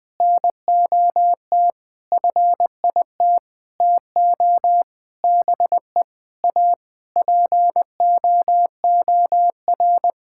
Morse code, heard as a keyed tone, about 20 wpm, 700 Hz. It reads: NOT FIT TO BE A POOR
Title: Sentences from Top 1000 Words - Review Part 5 - 20wpm